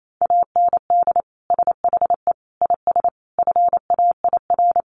Morse code, heard as a keyed tone, about 28 wpm, 700 Hz. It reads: AND SHE IS FAIR
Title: Sentences from Top 1300 Words 28wpm